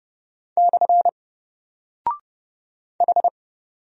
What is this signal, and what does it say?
Morse code, heard as a keyed tone, about 28 wpm, 700 Hz. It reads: / H
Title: Single Character (Letter:Number:Punctuation:Prosign) 30wpm